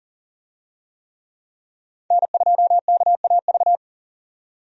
Morse code, heard as a keyed tone, about 40 wpm, 700 Hz. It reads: N2KAV
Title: USA Call Signs - 5 Characters - 40wpm